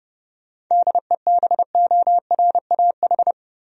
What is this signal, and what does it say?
Morse code, heard as a keyed tone, about 32 wpm, 700 Hz. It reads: DEBORAH
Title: USA Common Names - 30wpm